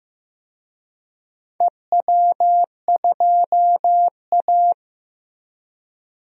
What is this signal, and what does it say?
Morse code, heard as a keyed tone, about 15 wpm, 700 Hz. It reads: EW2A
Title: DXCC-100 Call Signs - 4 Characters - Repeated 2x - 15wpm